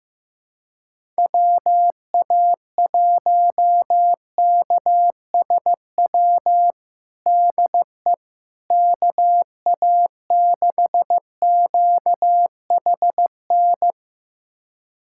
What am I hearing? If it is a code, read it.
Morse code, heard as a keyed tone, about 15 wpm, 700 Hz. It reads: WA1KSW DE KA6QHN
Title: USA Callsign DE Callsign - 6 Characters - 15wpm